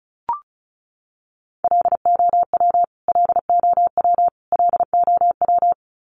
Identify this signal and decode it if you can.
Morse code, heard as a keyed tone, about 34 wpm, 700 Hz. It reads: LOW LOW LOW
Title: Top 200 Words Repeated 35wpm